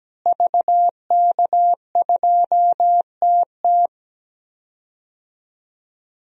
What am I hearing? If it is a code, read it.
Morse code, heard as a keyed tone, about 17 wpm, 700 Hz. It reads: VK2TT
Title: DXCC-100 Call Signs - 5 Characters - Repeated 3x - 17wpm